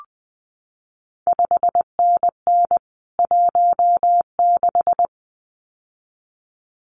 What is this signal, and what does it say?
Morse code, heard as a keyed tone, about 20 wpm, 700 Hz. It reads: 5NN 16
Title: CQ WW DX Contest Exchange - 20wpm